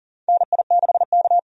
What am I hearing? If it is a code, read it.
Morse code, heard as a keyed tone, about 40 wpm, 700 Hz. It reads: NI6K
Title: USA Call Signs - 4 Characters - 40wpm